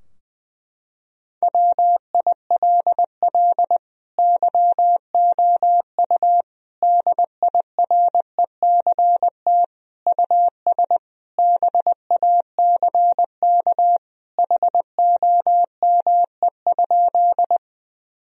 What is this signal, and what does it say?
Morse code, heard as a keyed tone, about 20 wpm, 700 Hz. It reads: WILL YOU DIRECT US BACK HOME?
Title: Sentences from Top 1000 Words - Review Part 2 - 20wpm